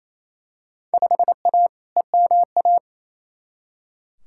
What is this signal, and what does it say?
Morse code, heard as a keyed tone, about 28 wpm, 700 Hz. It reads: 5A EMA
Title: Field Day Exchanges Encore - 28wpm